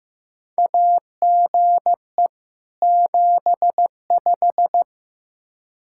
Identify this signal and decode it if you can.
Morse code, heard as a keyed tone, about 15 wpm, 700 Hz. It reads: AGE 75
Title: Practice QSO - Exchange 3 - More Info - 15wpm